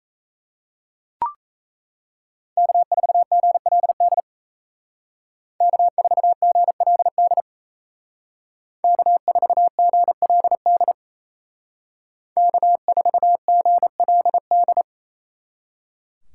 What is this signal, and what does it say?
Morse code, heard as a keyed tone, about 36 wpm, 700 Hz. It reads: K4GLD K4GLD K4GLD K4GLD